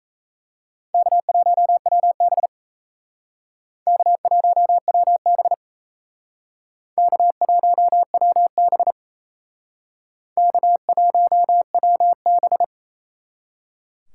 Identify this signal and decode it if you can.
Morse code, heard as a keyed tone, about 36 wpm, 700 Hz. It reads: K1WB K1WB K1WB K1WB